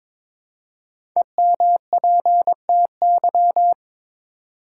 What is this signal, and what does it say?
Morse code, heard as a keyed tone, about 22 wpm, 700 Hz. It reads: EMPTY
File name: 5 Letter Words - First 3 Letters Given - 22wpm